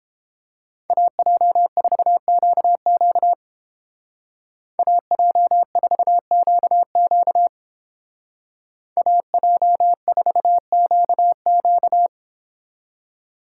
Morse code, heard as a keyed tone, about 30 wpm, 700 Hz. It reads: AJ4QQ AJ4QQ AJ4QQ